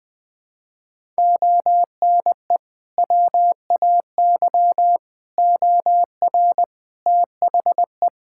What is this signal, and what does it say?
Morse code, heard as a keyed tone, about 20 wpm, 700 Hz. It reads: ONE WAY OR THE
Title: Build A Phrase - 5-Word Phrases - 20wpm